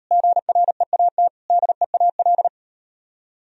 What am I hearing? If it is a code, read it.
Morse code, heard as a keyed tone, about 37 wpm, 700 Hz. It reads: GREAT DEAL